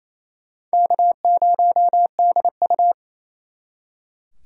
Morse code, heard as a keyed tone, about 36 wpm, 700 Hz. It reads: K0DU